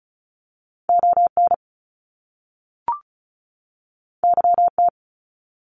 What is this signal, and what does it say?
Morse code, heard as a keyed tone, about 36 wpm, 700 Hz. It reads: ON YT
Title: Canadian Provinces and Territories 35wpm